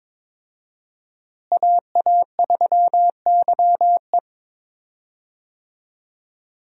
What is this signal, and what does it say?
Morse code, heard as a keyed tone, about 21 wpm, 700 Hz. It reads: AA3YE
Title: USA Call Signs - 5 Characters - Repeated 3x - 22wpm